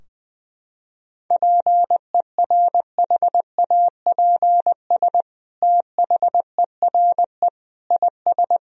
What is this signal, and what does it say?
Morse code, heard as a keyed tone, about 20 wpm, 700 Hz. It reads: PERHAPS THERE IS
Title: Sentences from Top 1000 Words - Review Part 4 - 20wpm